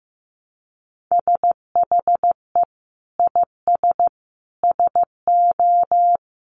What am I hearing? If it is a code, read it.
Morse code, heard as a keyed tone, about 15 wpm, 700 Hz. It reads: SHE IS SO
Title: Sentences from Top 1000 Words 15wpm